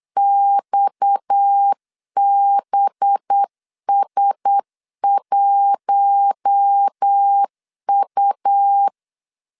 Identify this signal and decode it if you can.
Morse code, heard as a keyed tone, about 8 wpm, 800 Hz. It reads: XBS1U
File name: CW / DET / 1 6 1500C S B 8wpm 5142